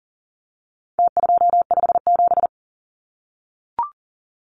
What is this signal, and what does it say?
Morse code, heard as a keyed tone, about 37 wpm, 700 Hz. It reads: T257
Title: Numbers - Contest Style - 4-Digit 40wpm